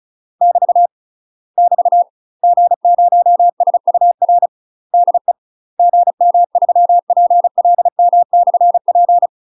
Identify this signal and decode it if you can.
Morse code, heard as a keyed tone, about 36 wpm, 700 Hz. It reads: <BT> <BT> G0SUR DE GM3PLM/P
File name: UK MASTER QSOs TRAINING / 35WPM / UK MASTER QSOs 0007